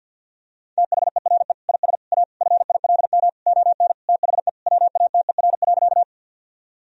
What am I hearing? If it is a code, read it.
Morse code, heard as a keyed tone, about 50 wpm, 700 Hz. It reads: THERE IS A FILM ON THE WATER.
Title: Sentences from Top 1000 Words - Review Part 3 - 50wpm